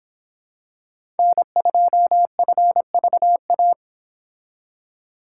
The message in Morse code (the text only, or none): N2FVA